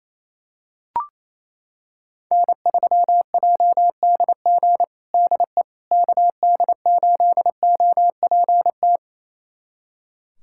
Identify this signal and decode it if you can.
Morse code, heard as a keyed tone, about 28 wpm, 700 Hz. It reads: N3JDG DE KD8OPT
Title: USA Callsign DE Callsign 28wpm